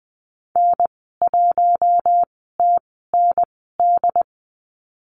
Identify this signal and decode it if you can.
Morse code, heard as a keyed tone, about 20 wpm, spaced as for 15 wpm, 700 Hz. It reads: N1TND
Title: 088 Lesson 15g - Number 5 - Call Signs 20wpm